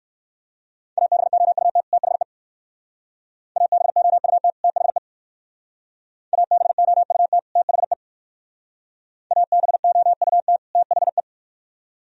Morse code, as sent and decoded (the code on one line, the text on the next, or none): .- -... --- ..- - / - .... / .- -... --- ..- - / - .... / .- -... --- ..- - / - .... / .- -... --- ..- - / - .... .
ABOUT THE ABOUT THE ABOUT THE ABOUT THE